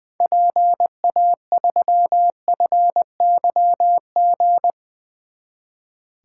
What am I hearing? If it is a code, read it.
Morse code, heard as a keyed tone, about 20 wpm, 700 Hz. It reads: PA3FYG
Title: 2023 SOTA Activator Call Signs - 20wpm